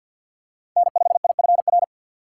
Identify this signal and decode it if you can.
Morse code, heard as a keyed tone, about 50 wpm, 690 Hz. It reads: N5IFR
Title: USA Call Signs - 5 Characters - 6x Character Spacing - 50wpm